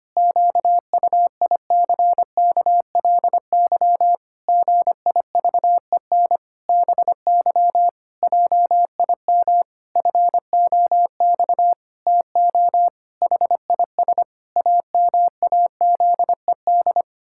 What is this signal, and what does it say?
Morse code, heard as a keyed tone, about 25 wpm, 700 Hz. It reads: QUICKLY GIVEN BY JIM FOX TO HIS AMAZED
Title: Pangram Challenge v2 - 25wpm